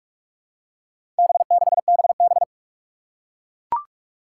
Morse code, heard as a keyed tone, about 44 wpm, 700 Hz. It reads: B6BB
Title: Four Challenging Letter-Number Combinations 45wpm